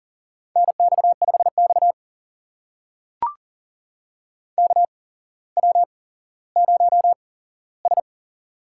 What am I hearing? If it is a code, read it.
Morse code, heard as a keyed tone, about 42 wpm, spaced as for 17 wpm, 700 Hz. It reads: NX5X KW0S
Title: USA Call Signs - 4 Characters - 8x Character Spacing - 40wpm